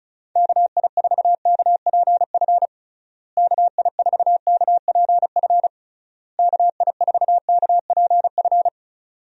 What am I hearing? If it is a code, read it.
Morse code, heard as a keyed tone, about 36 wpm, 700 Hz. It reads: KI4KPF KI4KPF KI4KPF
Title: USA Call Signs - 6 Characters - Repeated 3x - 35wpm